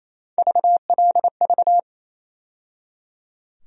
Morse code, heard as a keyed tone, about 27 wpm, 700 Hz. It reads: VLV
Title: USA Call Sign Postfixes - 3 Letters 28wpm